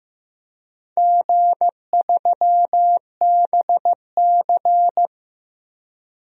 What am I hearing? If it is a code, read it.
Morse code, heard as a keyed tone, about 15 wpm, 700 Hz. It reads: G3BC
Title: DXCC-100 Call Signs - 4 Characters - Repeated 3x - 15wpm